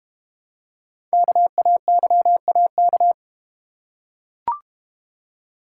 Morse code, heard as a keyed tone, about 32 wpm, 700 Hz. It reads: KAYAK